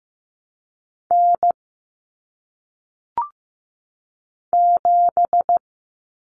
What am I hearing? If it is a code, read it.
Morse code, heard as a keyed tone, about 15 wpm, 700 Hz. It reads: N 7
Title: Common Cut Numbers - 1-Digit 15wpm